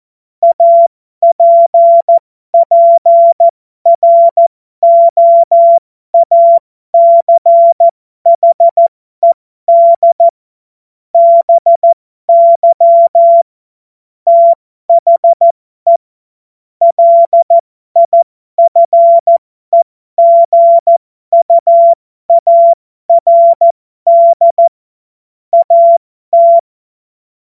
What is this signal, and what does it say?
Morse code, heard as a keyed tone, about 14 wpm, spaced as for 12 wpm, 680 Hz. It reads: APPROACHED BY THE LIFEGUARD AT
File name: Morsekurs / Witze